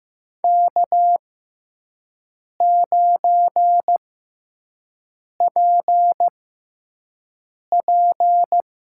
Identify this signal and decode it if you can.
Morse code, heard as a keyed tone, about 15 wpm, spaced as for 5 wpm, 700 Hz. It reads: K9PP
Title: USA Call Signs - 5 Characters - 6x Character Spacing - 15wpm